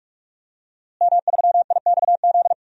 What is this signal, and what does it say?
Morse code, heard as a keyed tone, about 42 wpm, 700 Hz. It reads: M3IXZ
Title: DXCC-100 Call Signs - 5 Characters - Repeated 3x - 45wpm